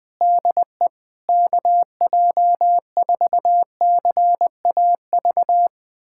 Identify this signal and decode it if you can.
Morse code, heard as a keyed tone, about 21 wpm, 700 Hz. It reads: DE KJ4CAV
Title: USA Callsign DE Callsign 20wpm